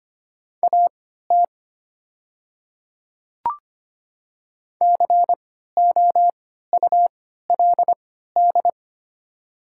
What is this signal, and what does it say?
Morse code, heard as a keyed tone, about 25 wpm, spaced as for 14 wpm, 690 Hz. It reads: AT COULD
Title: 3x Character Spacing - Top 100 Words - 25wpm